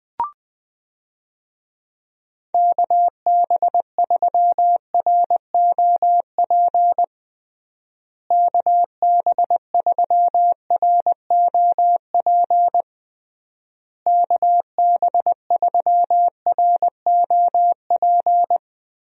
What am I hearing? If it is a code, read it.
Morse code, heard as a keyed tone, about 20 wpm, 700 Hz. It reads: KB3ROP KB3ROP KB3ROP
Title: USA Call Signs - 6 Characters - Repeated 3x - 20wpm